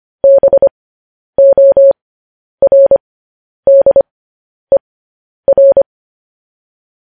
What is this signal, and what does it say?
Morse code, heard as a keyed tone, about 25 wpm, spaced as for 10 wpm, 550 Hz. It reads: BORDER
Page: news headlines in morse code at 10 wpm